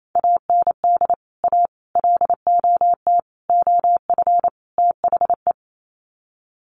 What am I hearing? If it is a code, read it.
Morse code, heard as a keyed tone, about 28 wpm, 700 Hz. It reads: AND A LOT OF THE
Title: Sets of 5 Words Encore 28wpm